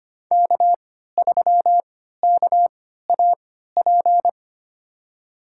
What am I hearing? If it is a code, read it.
Morse code, heard as a keyed tone, about 25 wpm, spaced as for 14 wpm, 700 Hz. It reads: K3KAP
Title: USA Call Signs - 3x Character Spacing - 25wpm